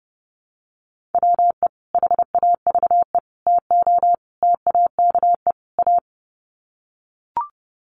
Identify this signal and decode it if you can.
Morse code, heard as a keyed tone, about 30 wpm, 700 Hz. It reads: WE HAVE TO TAKE A
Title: Sets of 5 Words Encore 30wpm